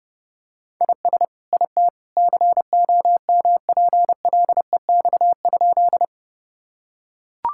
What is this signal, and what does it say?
Morse code, heard as a keyed tone, about 29 wpm, 700 Hz. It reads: IS IT COMPLEX?
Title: Sentences from Top 1000 Words - Review Part 6 - 30wpm